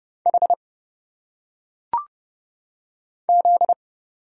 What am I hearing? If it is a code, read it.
Morse code, heard as a keyed tone, about 30 wpm, 700 Hz. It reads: H Z
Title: Single Letters - Phonetic Pronunciation 30wpm